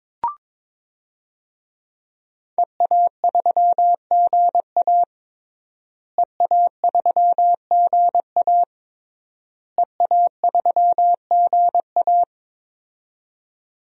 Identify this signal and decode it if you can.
Morse code, heard as a keyed tone, about 21 wpm, 700 Hz. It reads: EA3GA EA3GA EA3GA
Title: DXCC-100 Call Signs - 5 Characters - Repeated 3x - 22wpm